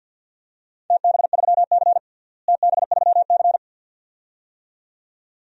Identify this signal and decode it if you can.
Morse code, heard as a keyed tone, about 48 wpm, 700 Hz. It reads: TB3C TB3C
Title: DXCC-100 Call Signs - 4 Characters - Repeated 2x - 50wpm